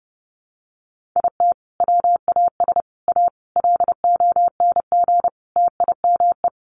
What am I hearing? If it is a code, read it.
Morse code, heard as a keyed tone, about 30 wpm, 700 Hz. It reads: IT WAS A LONG TIME
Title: Sentences from Top 500 Words 30wpm